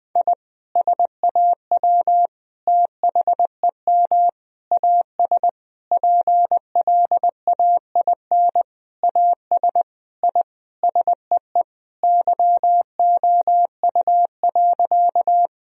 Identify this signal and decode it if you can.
Morse code, heard as a keyed tone, about 19 wpm, 700 Hz. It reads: I SAW THEM AS PLAIN AS I SEE YOU.
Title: Sentences from Top 1000 Words - Review Part 2 - 20wpm